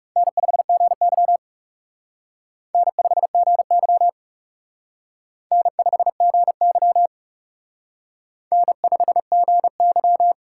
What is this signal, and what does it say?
Morse code, heard as a keyed tone, about 36 wpm, 700 Hz. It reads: N5GY N5GY N5GY N5GY
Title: Speed Racing - USA 1x2 Call Signs 30wpm